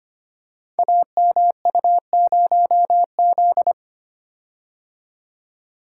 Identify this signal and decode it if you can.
Morse code, heard as a keyed tone, about 25 wpm, 700 Hz. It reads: AMU0Z
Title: Code Groups - 5 Letter:Numbers - 25wpm